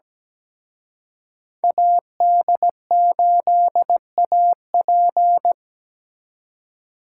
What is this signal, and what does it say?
Morse code, heard as a keyed tone, about 20 wpm, 700 Hz. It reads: AD8AP
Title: Speed Racing - USA 2x2 Call Signs 17wpm